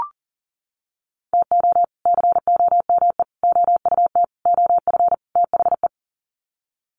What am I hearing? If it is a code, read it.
Morse code, heard as a keyed tone, about 40 wpm, 700 Hz. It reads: TO COME OUT OF THE
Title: Sets of 5 Words Encore2 40wpm